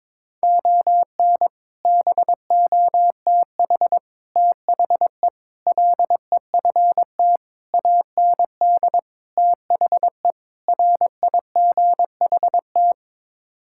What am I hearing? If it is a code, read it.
Morse code, heard as a keyed tone, about 22 wpm, 700 Hz. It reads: ON BOTH THE LEFT AND THE RIGHT
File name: practice / 22wpm / Sets of 7 Words - 22wpm